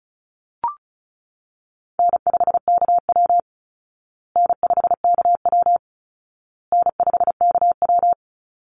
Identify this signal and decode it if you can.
Morse code, heard as a keyed tone, about 34 wpm, 700 Hz. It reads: N5KW N5KW N5KW
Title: Easy Call Signs Repeated 3x - 35wpm